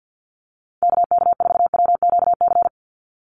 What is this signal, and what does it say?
Morse code, heard as a keyed tone, about 50 wpm, 700 Hz. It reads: KK4WQC
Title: USA Call Signs 50wpm